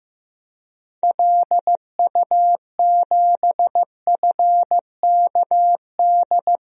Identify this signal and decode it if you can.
Morse code, heard as a keyed tone, about 15 wpm, 700 Hz. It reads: LU7FKD
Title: DXCC-100 Call Signs - 6 Characters - Repeated 2x - 15wpm